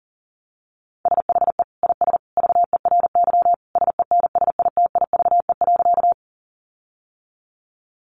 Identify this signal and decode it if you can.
Morse code, heard as a keyed tone, about 37 wpm, 700 Hz. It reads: SHE IS VERY SENSITIVE.
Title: Sentences from Top 1400 Words 40wpm